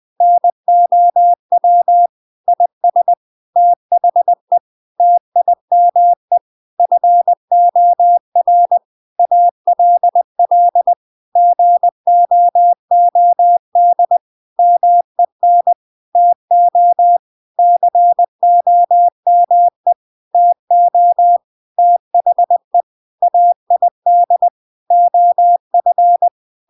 Morse code, heard as a keyed tone, about 20 wpm, 700 Hz. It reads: NOW IS THE TIME FOR ALL GOOD MEN TO COME TO THE AID OF
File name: CW Learning Programs / sounds and personalities of cw and transmitters / 14 - Now20